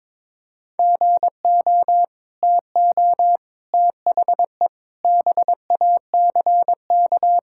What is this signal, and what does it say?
Morse code, heard as a keyed tone, about 22 wpm, 700 Hz. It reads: GO TO THE BACK
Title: Sentences from Top 1000 Words - Review Part 5 - 22wpm